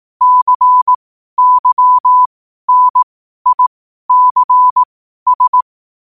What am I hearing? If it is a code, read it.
Morse code, heard as a keyed tone, about 18 wpm, spaced as for 12 wpm, 1000 Hz. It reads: CYNICS